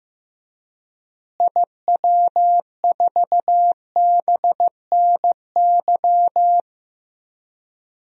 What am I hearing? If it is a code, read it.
Morse code, heard as a keyed tone, about 15 wpm, 700 Hz. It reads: IW4BNY
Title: DXCC-100 Call Signs - 6 Characters - Repeated 3x - 15wpm